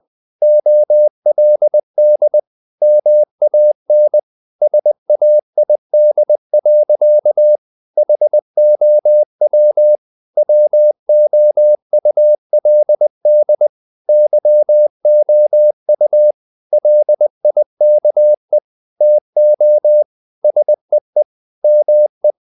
Morse code, heard as a keyed tone, about 20 wpm, 600 Hz. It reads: OLD MAN SAID. HOW WOULD YOU LIKE TO SEE ME
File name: the old man and the sea / the old man and the sea part 1 20wpm